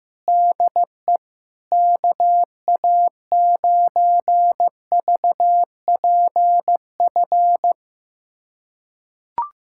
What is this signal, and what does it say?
Morse code, heard as a keyed tone, about 15 wpm, 700 Hz. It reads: DE KA9VPF
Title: USA Callsign DE Callsign - 6 Characters - 15wpm